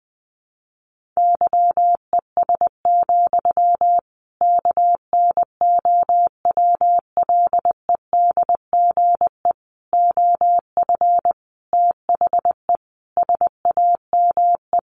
Morse code, heard as a keyed tone, about 20 wpm, 700 Hz. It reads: YES, KNOWLEDGE OF THE SAME
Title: Sentences from Top 1500 Words 20wpm